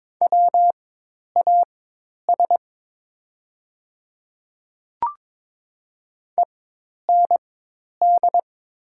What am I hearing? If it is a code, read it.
Morse code, heard as a keyed tone, about 22 wpm, spaced as for 10 wpm, 700 Hz. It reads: WAS END
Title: 4x Character Spacing - Top 200 Words - 22wpm